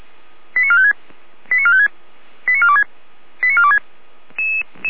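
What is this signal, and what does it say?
Five-tone selective call (ZVEI), heard as quick runs of five tones, tones of about 70 ms. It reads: 89456 89456 89426 89426
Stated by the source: ZVEI five-tone call received by SDR.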